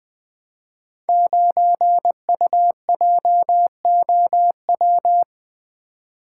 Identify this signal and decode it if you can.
Morse code, heard as a keyed tone, about 21 wpm, 700 Hz. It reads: 9UJOW
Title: Code Groups - 5 Letter:Numbers - 20wpm